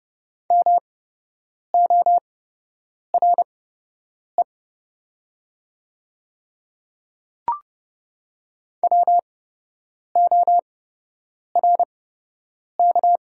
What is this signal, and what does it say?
Morse code, heard as a keyed tone, about 30 wpm, spaced as for 8 wpm, 690 Hz. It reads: MORE WORK
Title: 8x Character Spacing - Top 500 Words - 30wpm